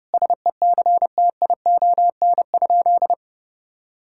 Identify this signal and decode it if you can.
Morse code, heard as a keyed tone, about 32 wpm, 700 Hz. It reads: SECTION?
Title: Sentences from Top 1000 Words - Review Part 5 - 30wpm